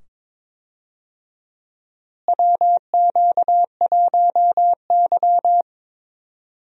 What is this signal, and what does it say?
Morse code, heard as a keyed tone, about 22 wpm, 700 Hz. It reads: WQ1Y
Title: USA Call Signs - 4 Characters - Repeated 2x - 22wpm